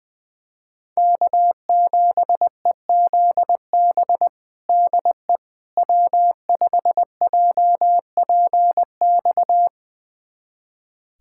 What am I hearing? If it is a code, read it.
Morse code, heard as a keyed tone, about 21 wpm, 700 Hz. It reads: K7EZB DE W5JPX
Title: USA Callsign DE Callsign - 5 Characters - 20wpm